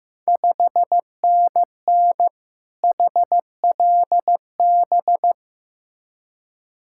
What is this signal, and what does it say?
Morse code, heard as a keyed tone, about 15 wpm, 690 Hz. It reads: 5NN HLB
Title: Ontario QSO Party Exchange 15wpm